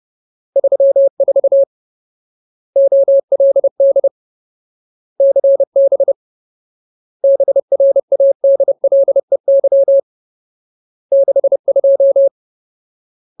Morse code, heard as a keyed tone, about 30 wpm, 550 Hz. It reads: 34 OLD CB BRADLEY 62
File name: Ultimate Word List / 30WPM Ultimate